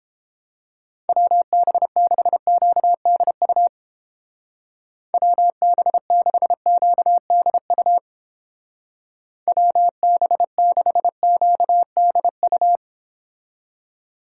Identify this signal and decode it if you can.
Morse code, heard as a keyed tone, about 29 wpm, 700 Hz. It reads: WB6QDU WB6QDU WB6QDU